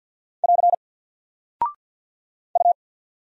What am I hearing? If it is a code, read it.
Morse code, heard as a keyed tone, about 48 wpm, 700 Hz. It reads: <AR> U
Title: Single Character (Letter:Number:Punctuation:Prosign) 50wpm